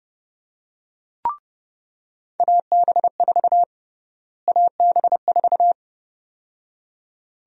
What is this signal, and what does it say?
Morse code, heard as a keyed tone, about 29 wpm, 700 Hz. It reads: AB4 AB4
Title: USA Call Sign Prefixes - Repeated 2x - 30wpm